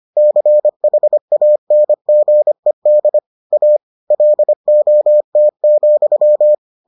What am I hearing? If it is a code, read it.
Morse code, heard as a keyed tone, about 25 wpm, 600 Hz. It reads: CHANGED A LOT,